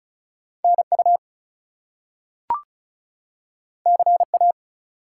Morse code, heard as a keyed tone, about 35 wpm, 700 Hz. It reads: NU CA
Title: US States and Canadian Provinces 35wpm